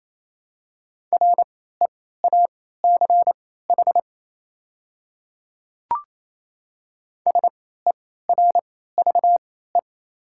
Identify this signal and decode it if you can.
Morse code, heard as a keyed tone, about 27 wpm, spaced as for 16 wpm, 700 Hz. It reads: REACH SERVE
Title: 3x Character Spacing - Top 500 Words - 28wpm